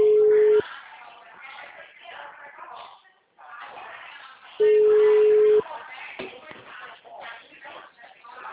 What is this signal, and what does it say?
Ringback tone (Europe, 425 Hz), with speech.
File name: out- -2010- - - .40216